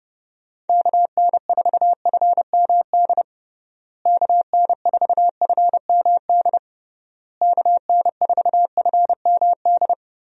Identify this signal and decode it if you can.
Morse code, heard as a keyed tone, about 32 wpm, 700 Hz. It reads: KN4FMD KN4FMD KN4FMD
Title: USA Call Signs - 6 Characters - Repeated 3x - 30wpm